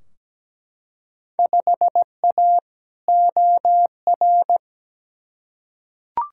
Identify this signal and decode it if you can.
Morse code, heard as a keyed tone, about 17 wpm, 700 Hz. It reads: 5A OR
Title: Field Day Exchanges - 17wpm